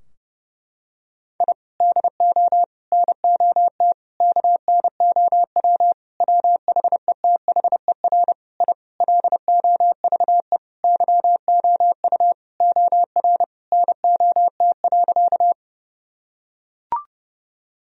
Morse code, heard as a keyed tone, about 30 wpm, 700 Hz. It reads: I DO NOT KNOW WHETHER I LOVE YOU OR NOT.
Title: Sentences from Top 1000 Words - Review Part 3 - 30wpm